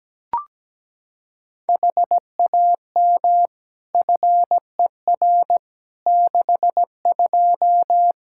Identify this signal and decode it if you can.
Morse code, heard as a keyed tone, about 17 wpm, 700 Hz. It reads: HAM FER 62
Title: K1USN SST QSO Practice 17wpm